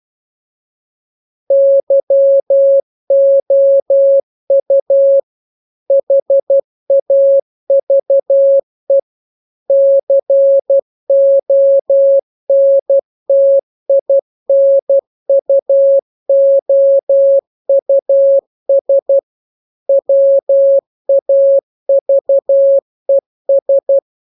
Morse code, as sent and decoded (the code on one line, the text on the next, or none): -.-- --- ..- / .... .- ...- . / -.-. --- -. - .. -. ..- --- ..- ... / .-- .- ...- . ...
YOU HAVE CONTINUOUS WAVES